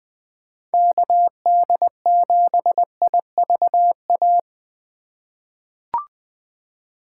Morse code, heard as a keyed tone, about 20 wpm, 700 Hz. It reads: KD7IVA
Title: USA Call Signs - 2x Character Spacing - 20wpm